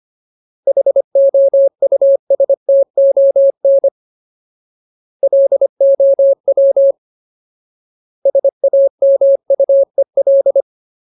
Morse code, heard as a keyed tone, about 25 wpm, 550 Hz. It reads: HOUSTON LOW SAMUEL
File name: Ultimate Word List / 25WPM Ultimate